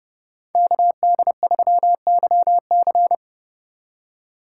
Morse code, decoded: KD3YC

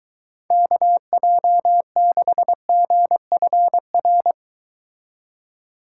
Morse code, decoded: KJ6GFR

